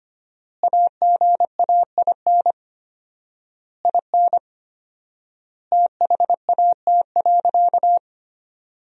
Morse code, heard as a keyed tone, about 25 wpm, 700 Hz. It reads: AGAIN IN THAT.
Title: 4x Word-Spacing - Sentences from Top 200 Words - 25wpm